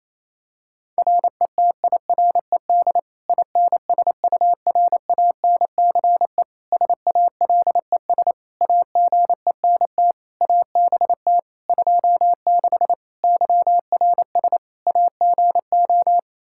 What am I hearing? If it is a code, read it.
Morse code, heard as a keyed tone, about 28 wpm, 700 Hz. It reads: RETIRED INSURANCE SALES AGENT ABT 26 YRS AGO
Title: Practice QSO - Exchange 3 - More Info - 28wpm